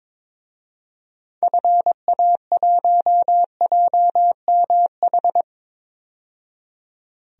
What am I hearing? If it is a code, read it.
Morse code, heard as a keyed tone, about 22 wpm, 700 Hz. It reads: FA1JMH
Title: DXCC-100 Call Signs - 6 Characters - 22wpm